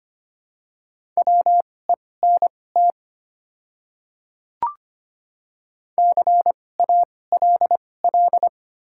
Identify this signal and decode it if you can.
Morse code, heard as a keyed tone, about 25 wpm, spaced as for 18 wpm, 700 Hz. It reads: WENT CALL
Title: 2x Character Spacing - Top 200 Words - 25wpm